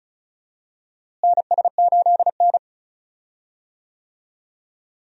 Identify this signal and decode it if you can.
Morse code, heard as a keyed tone, about 34 wpm, 700 Hz. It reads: NS8N
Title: USA Call Signs - 4 Characters - Repeated 3x - 35wpm